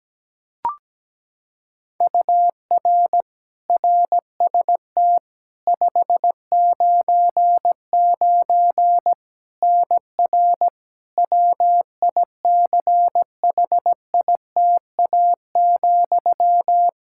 Morse code, heard as a keyed tone, about 17 wpm, 700 Hz. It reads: UR RST 599 NR WICHITA,
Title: K1USN SST QSO Practice 17wpm